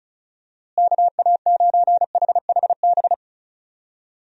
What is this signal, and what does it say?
Morse code, heard as a keyed tone, about 36 wpm, 700 Hz. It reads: KA9HHB
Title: USA Call Signs - 6 Characters - Repeated 3x - 35wpm